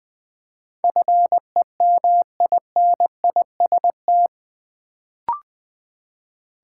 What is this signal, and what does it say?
Morse code, heard as a keyed tone, about 20 wpm, 700 Hz. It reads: FEMINIST
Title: Build A Word - 4 Letters At A Time - 8-Letter Words - 20wpm